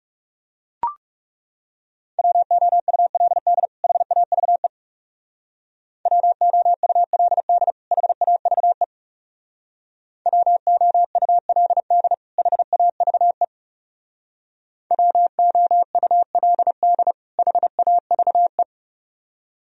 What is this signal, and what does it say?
Morse code, heard as a keyed tone, about 35 wpm, 700 Hz. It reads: WOULD HAVE WOULD HAVE WOULD HAVE WOULD HAVE
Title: Speed-Racing - Sets of 2 Words 30wpm